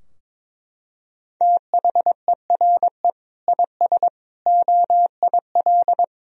Morse code, heard as a keyed tone, about 22 wpm, 700 Hz. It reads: THERE IS OIL
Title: Sentences from Top 1000 Words - Review Part 4 - 22wpm